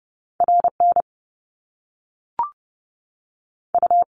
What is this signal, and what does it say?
Morse code, heard as a keyed tone, about 30 wpm, 700 Hz. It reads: RN U
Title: Top Two Letter Combinations in Words 30wpm